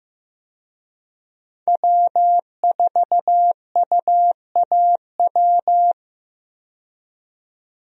Morse code, heard as a keyed tone, about 15 wpm, 700 Hz. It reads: W4UAW
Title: USA Call Signs - 5 Characters - 15wpm